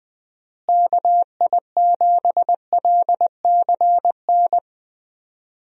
Morse code, heard as a keyed tone, about 21 wpm, 700 Hz. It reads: KI7LCN
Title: USA Call Signs - 6 Characters - Repeated 2x - 20wpm